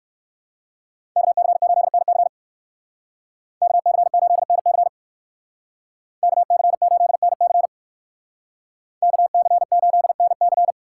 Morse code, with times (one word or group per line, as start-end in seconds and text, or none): 1.2-2.3 KC8NC
3.6-4.9 KC8NC
6.2-7.7 KC8NC
9.0-10.7 KC8NC